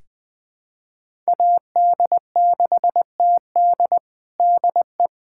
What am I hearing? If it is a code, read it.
Morse code, heard as a keyed tone, about 21 wpm, 700 Hz. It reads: AD6TD DE
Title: USA Callsign DE Callsign - 5 Characters - 20wpm